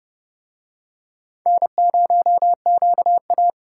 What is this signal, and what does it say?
Morse code, heard as a keyed tone, about 29 wpm, 700 Hz. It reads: N0QA